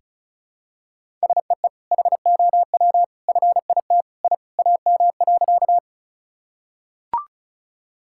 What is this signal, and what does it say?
Morse code, heard as a keyed tone, about 34 wpm, 700 Hz. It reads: SEE HOW FIT I AM.
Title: Sentences from Top 1000 Words - Review Part 5 - 35wpm